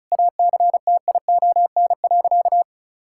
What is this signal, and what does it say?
Morse code, heard as a keyed tone, about 34 wpm, 700 Hz. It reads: ACTION.
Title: Sentences from Top 1000 Words - Review Part 5 - 35wpm